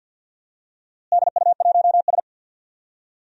Morse code, heard as a keyed tone, about 46 wpm, 700 Hz. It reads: DU1S
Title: DXCC-100 Call Signs - 4 Characters - Repeated 2x - 50wpm